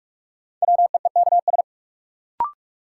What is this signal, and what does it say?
Morse code, heard as a keyed tone, about 47 wpm, 700 Hz. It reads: WEEKS
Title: Sets of 5 Words - New Word Drill - 45wpm